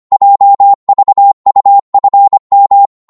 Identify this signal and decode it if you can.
Morse code, heard as a keyed tone, about 25 wpm, 800 Hz. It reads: JVUFM
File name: Morse / ltr-25wpm